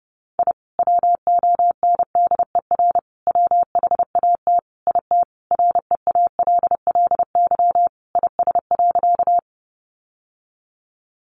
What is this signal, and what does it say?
Morse code, heard as a keyed tone, about 30 wpm, 700 Hz. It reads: I WONDER WHAT IT REALLY IS.
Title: Sentences from Top 600 Words 30wpm